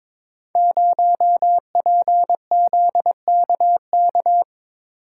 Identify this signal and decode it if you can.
Morse code, heard as a keyed tone, about 22 wpm, 700 Hz. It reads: 0PZKK
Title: Code Groups - 5 Letter:Numbers - 22wpm